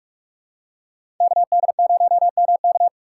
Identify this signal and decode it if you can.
Morse code, heard as a keyed tone, about 46 wpm, 700 Hz. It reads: KD0MK
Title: USA Call Signs - 5 Characters - 3x Character Spacing - 45wpm